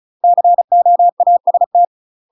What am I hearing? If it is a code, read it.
Morse code, heard as a keyed tone, about 36 wpm, 700 Hz. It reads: COAST
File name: UK MASTER QSOs TRAINING / 35WPM / UK MASTER QSOs 0017